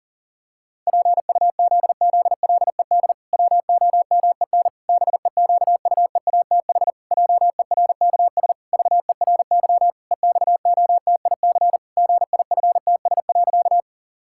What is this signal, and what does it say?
Morse code, heard as a keyed tone, about 40 wpm, 700 Hz. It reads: PUZZLED WOMEN BEQUEATH JERKS VERY EXOTIC GIFTS.
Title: Pangram Challenge v2 - 40wpm